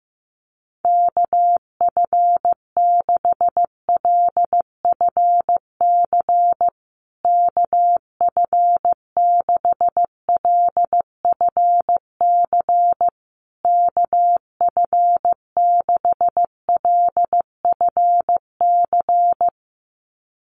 Morse code, heard as a keyed tone, about 15 wpm, 700 Hz. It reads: KF6LFC KF6LFC KF6LFC
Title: USA Call Signs Repeated 3x 15wpm